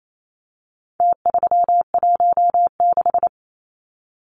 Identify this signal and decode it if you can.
Morse code, heard as a keyed tone, about 28 wpm, 700 Hz. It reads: T316
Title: Numbers - Contest Style - 4-Digit 28wpm